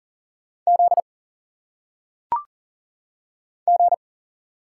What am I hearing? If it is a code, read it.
Morse code, heard as a keyed tone, about 40 wpm, 700 Hz. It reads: Z G
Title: Single Letters - Phonetic Pronunciation 40wpm